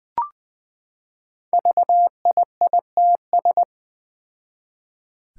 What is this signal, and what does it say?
Morse code, heard as a keyed tone, about 20 wpm, 700 Hz. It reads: VIITS
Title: Five Challenging Letter-Number Combinations 20wpm